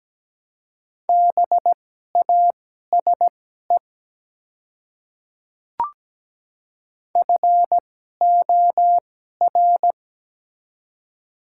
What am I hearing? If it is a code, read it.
Morse code, heard as a keyed tone, about 17 wpm, spaced as for 12 wpm, 700 Hz. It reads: BASE FOR
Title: 2x Character Spacing - Top 300 Words - 17wpm